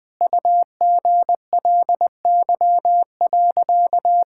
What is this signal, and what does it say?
Morse code, heard as a keyed tone, about 20 wpm, 700 Hz. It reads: UGLY.
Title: Sentences with 7-Word Phrases - 20wpm